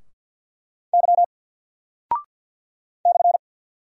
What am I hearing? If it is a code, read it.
Morse code, heard as a keyed tone, about 48 wpm, 700 Hz. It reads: Y /
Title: Single Character (Letter:Number:Punctuation:Prosign) Alt 50wpm